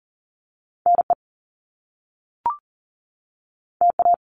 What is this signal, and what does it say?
Morse code, heard as a keyed tone, about 39 wpm, 700 Hz. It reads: NE TA
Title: Top Two Letter Combinations in Words 40wpm